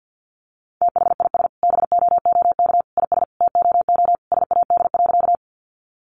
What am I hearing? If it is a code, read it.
Morse code, heard as a keyed tone, about 50 wpm, 700 Hz. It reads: THIS BOOK IS TOO SAD.
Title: Additional Easy Phrases 50wpm